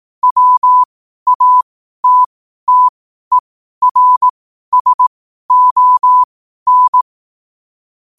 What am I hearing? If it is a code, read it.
Morse code, heard as a keyed tone, about 18 wpm, spaced as for 12 wpm, 1000 Hz. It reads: WATTERSON
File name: qotdfiles / 12wpm